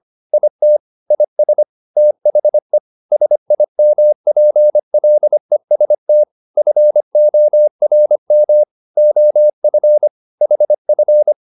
Morse code, heard as a keyed tone, about 25 wpm, 600 Hz. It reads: IT IS THE SIMPLEST FORM OF HF